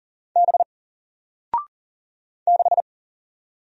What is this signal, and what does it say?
Morse code, heard as a keyed tone, about 40 wpm, 700 Hz. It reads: B 6
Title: Single Character (Letter:Number:Punctuation:Prosign) 40wpm